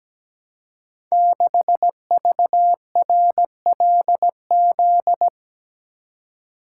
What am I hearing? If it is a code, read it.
Morse code, heard as a keyed tone, about 17 wpm, 700 Hz. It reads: 6VRLZ
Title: Code Groups - 5 Letter:Numbers - 17wpm